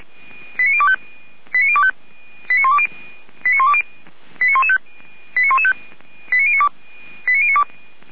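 Five-tone selective call (ZVEI), heard as quick runs of five tones, tones of about 70 ms. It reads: 89025 89025 89120 89120 89115 89115 89092 89092
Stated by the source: ZVEI five-tone selective call 89280 received via SDR.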